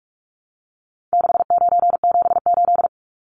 Morse code, heard as a keyed tone, about 44 wpm, 700 Hz. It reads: 6978
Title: Numbers - 4-Digit 45wpm